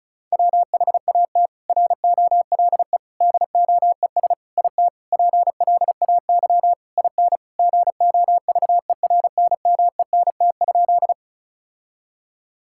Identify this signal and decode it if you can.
Morse code, heard as a keyed tone, about 34 wpm, 700 Hz. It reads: WHAT ROLE DOES IT PLAY IN GOVERNMENT?
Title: Sentences from Top 1000 Words - Review Part 6 - 35wpm